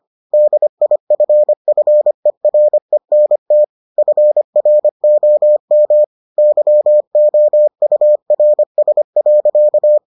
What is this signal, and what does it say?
Morse code, heard as a keyed tone, about 25 wpm, 600 Hz. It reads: DIFFERENT FROM YOURS.